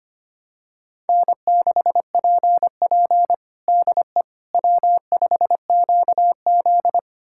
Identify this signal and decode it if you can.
Morse code, heard as a keyed tone, about 25 wpm, 700 Hz. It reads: N6PP DE W5QZ
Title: USA Callsign DE Callsign - 4 Characters - 25wpm